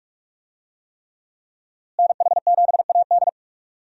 Morse code, heard as a keyed tone, about 44 wpm, 700 Hz. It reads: NH7AD
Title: USA Call Signs - 5 Characters - Repeated 3x - 45wpm